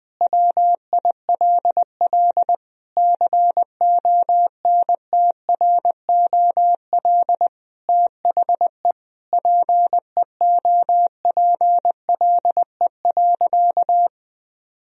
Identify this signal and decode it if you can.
Morse code, heard as a keyed tone, about 21 wpm, 700 Hz. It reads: WILL CONTROL THE PEOPLE.
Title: Sentences from Top 1000 Words - Review Part 3 - 20wpm